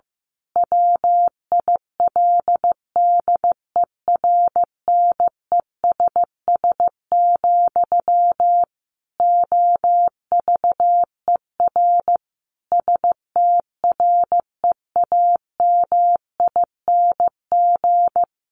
Morse code, heard as a keyed tone, about 15 wpm, 700 Hz. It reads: WILDERNESS, OVER STREAMING
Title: A Sound Of Thunder 15wpm